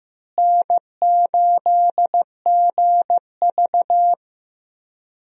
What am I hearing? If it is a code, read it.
Morse code, heard as a keyed tone, about 16 wpm, 700 Hz. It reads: N8GV
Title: USA Call Signs - 4 Characters - Repeated 2x - 15wpm